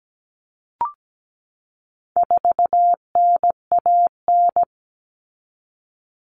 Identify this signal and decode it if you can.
Morse code, heard as a keyed tone, about 17 wpm, 700 Hz. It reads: 4NAN
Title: Common Cut Numbers - 4-Digit 17wpm